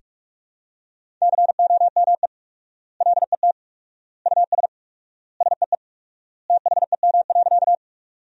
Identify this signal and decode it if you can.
Morse code, heard as a keyed tone, about 46 wpm, 700 Hz. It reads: COME LET US SEE THEM.
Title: 4x Word-Spacing - Sentences from Top 300 Words - 45wpm